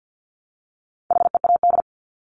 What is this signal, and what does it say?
Morse code, heard as a keyed tone, about 50 wpm, 700 Hz. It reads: HEAD
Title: Top 500 Words 50wpm